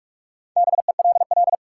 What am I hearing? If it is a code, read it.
Morse code, heard as a keyed tone, about 46 wpm, 700 Hz. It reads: BELL
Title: Four Letter Words - 45wpm